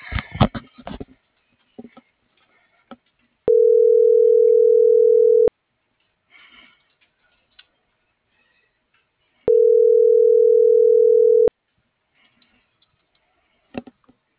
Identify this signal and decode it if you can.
Ringback tone (North America), 440 + 480 Hz.